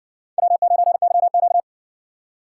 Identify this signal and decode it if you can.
Morse code, heard as a keyed tone, about 60 wpm, 700 Hz. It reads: W9YQ